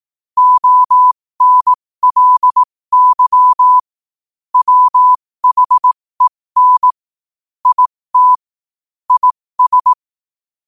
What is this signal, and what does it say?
Morse code, heard as a keyed tone, about 18 wpm, spaced as for 15 wpm, 1000 Hz. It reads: ONLY WHEN IT IS
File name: qotdfiles / 15wpm